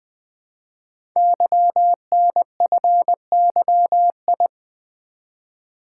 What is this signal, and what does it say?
Morse code, heard as a keyed tone, about 20 wpm, 700 Hz. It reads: YNFYI
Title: Code Groups - 5 Letters - 20wpm